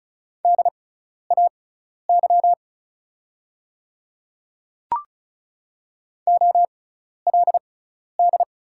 Morse code, heard as a keyed tone, about 34 wpm, spaced as for 12 wpm, 690 Hz. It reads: DAY OLD
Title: 6x Character Spacing - Top 200 Words - 35wpm